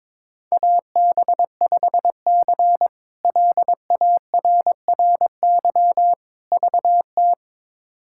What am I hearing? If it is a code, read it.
Morse code, heard as a keyed tone, about 22 wpm, 700 Hz. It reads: AB5C LARRY VT
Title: K1USN SST QSO Practice - 22wpm